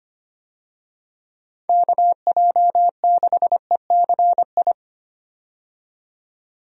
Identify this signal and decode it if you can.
Morse code, heard as a keyed tone, about 25 wpm, 700 Hz. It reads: KJ6ECI
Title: USA Call Signs - 6 Characters - 25wpm